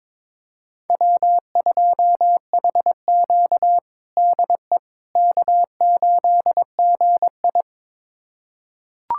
Morse code, heard as a keyed tone, about 22 wpm, 700 Hz. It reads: W2HQ DE K8GI
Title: USA Callsign DE Callsign - 4 Characters - 22wpm